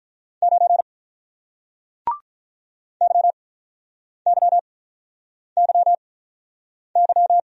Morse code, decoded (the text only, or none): 9 YYYY